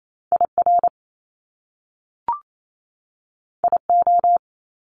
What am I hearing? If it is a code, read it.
Morse code, heard as a keyed tone, about 28 wpm, 700 Hz. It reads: IR IO
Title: Top Two Letter Combinations in Words 28wpm